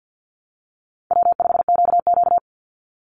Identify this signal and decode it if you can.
Morse code, heard as a keyed tone, about 48 wpm, 700 Hz. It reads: W5QQ
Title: Call Signs Repeated 3x - 50wpm